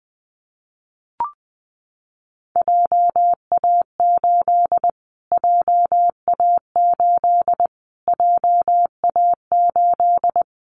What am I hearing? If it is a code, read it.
Morse code, heard as a keyed tone, about 19 wpm, 700 Hz. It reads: JA8 JA8 JA8
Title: Callsign Prefixes for Top 25 DXCC - Repeated 3x - 20wpm